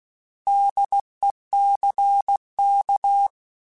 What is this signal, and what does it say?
Morse code, heard as a keyed tone, about 16 wpm, 800 Hz. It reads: DECK
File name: Alphabetical / Alphabetical mp3s / Ds @15